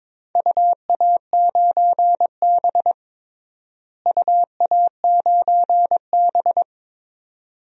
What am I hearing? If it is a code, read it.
Morse code, heard as a keyed tone, about 22 wpm, 700 Hz. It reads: UA9B UA9B
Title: DXCC-100 Call Signs - 4 Characters - Repeated 2x - 22wpm